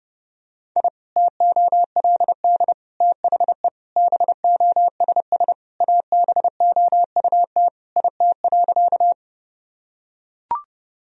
Morse code, heard as a keyed tone, about 30 wpm, 700 Hz. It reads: I TOLD THE BOSS ABOUT IT.
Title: Sentences from Top 1000 Words - Review Part 4 - 30wpm